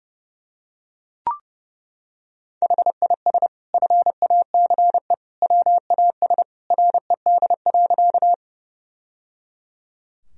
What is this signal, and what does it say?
Morse code, heard as a keyed tone, about 30 wpm, 700 Hz. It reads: HIS FACE WAS RED.
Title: Sentences from Top 1000 Words - Review Part 2 - 30wpm